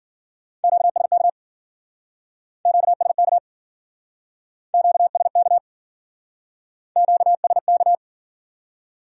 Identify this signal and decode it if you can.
Morse code, heard as a keyed tone, about 56 wpm, 700 Hz. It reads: QSK QSK QSK QSK